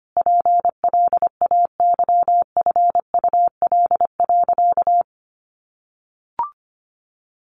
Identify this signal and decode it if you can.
Morse code, heard as a keyed tone, about 25 wpm, 700 Hz. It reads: PLAYFUL.
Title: Easy Phrases 25wpm